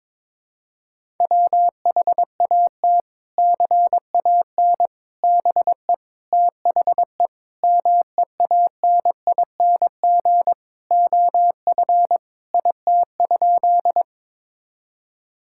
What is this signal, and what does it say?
Morse code, heard as a keyed tone, about 22 wpm, 700 Hz. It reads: WHAT CAN BE THE MEANING OF IT?
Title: Sentences from Top 1000 Words - Review Part 6 - 22wpm